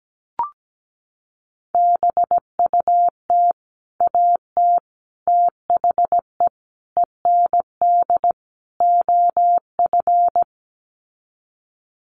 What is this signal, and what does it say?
Morse code, heard as a keyed tone, about 17 wpm, 700 Hz. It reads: BUT AT THE END OF
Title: Sets of 5 Words Encore2 17wpm